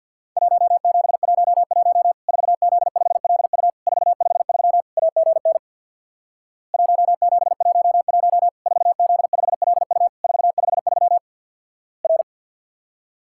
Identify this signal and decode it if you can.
Morse code, heard as a keyed tone, about 50 wpm, 700 Hz. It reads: 1711 4Z5LU 453 AGN 1711 4Z5LU 453 R
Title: Worked All Europe DX Contest - QTC Exchange with Repeats - 50wpm